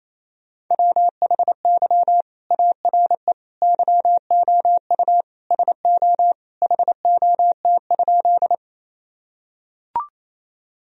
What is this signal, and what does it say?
Morse code, heard as a keyed tone, about 28 wpm, 700 Hz. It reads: WHY ARE YOU SO HOT?
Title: Sentences from Top 1000 Words - Review Part 3 - 28wpm